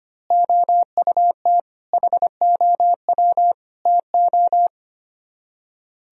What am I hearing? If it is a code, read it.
Morse code, heard as a keyed tone, about 25 wpm, 700 Hz. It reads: OUT HOW TO
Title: Speed-Racing - Sets of 5 Words 25wpm